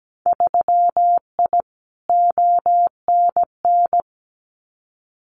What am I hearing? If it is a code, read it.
Morse code, heard as a keyed tone, about 17 wpm, 700 Hz. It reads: 3I ONN
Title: Winter Field Day Exchanges Encore 17wpm